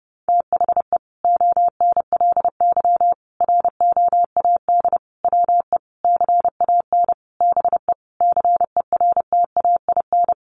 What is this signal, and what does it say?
Morse code, heard as a keyed tone, about 30 wpm, 700 Hz. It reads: THE ONLY ROAD WE CAN BE CERTAIN
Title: Sentences from Top 1100 Words 30wpm